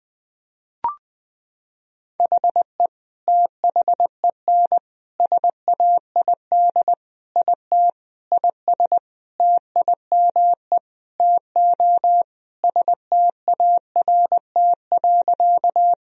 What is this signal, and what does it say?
Morse code, heard as a keyed tone, about 21 wpm, 700 Hz. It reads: HE THEN SAID IT IS TIME TO START.
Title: Sentences from Top 1000 Words - Review Part 1 - 20wpm